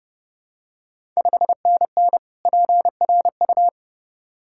Morse code, heard as a keyed tone, about 30 wpm, 700 Hz. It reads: 5NN PRU
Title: Ontario QSO Party Exchange 30wpm